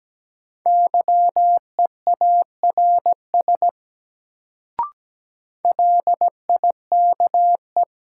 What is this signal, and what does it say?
Morse code, heard as a keyed tone, about 17 wpm, 700 Hz. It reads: YEARS LIKE
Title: Sets of 2 Words - New Word Drill - 17wpm